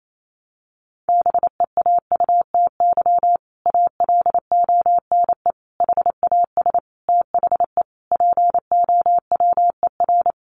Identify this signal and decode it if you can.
Morse code, heard as a keyed tone, about 28 wpm, 700 Hz. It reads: BEAUTY ALONE HAS THE POWER
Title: Sentences from Top 1200 Words 28wpm